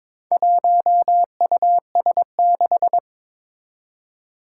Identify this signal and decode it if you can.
Morse code, heard as a keyed tone, about 22 wpm, 700 Hz. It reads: 1US6
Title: Four Challenging Letter-Number Combinations 22wpm